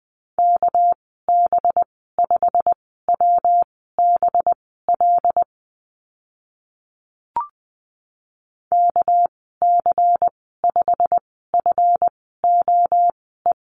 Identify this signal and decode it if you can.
Morse code, heard as a keyed tone, about 19 wpm, spaced as for 14 wpm, 700 Hz. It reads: KB5WBL KC5FOE
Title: 245 Lesson 39g - Letter z - Call Signs 20wpm